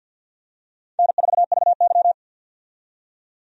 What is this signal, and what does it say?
Morse code, heard as a keyed tone, about 50 wpm, 700 Hz. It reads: N4VY
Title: USA Call Signs - 4 Characters - 50wpm